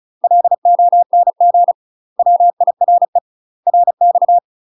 Morse code, heard as a keyed tone, about 36 wpm, 700 Hz. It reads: LONG WIRE RX
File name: UK MASTER QSOs TRAINING / 35WPM / UK MASTER QSOs 0038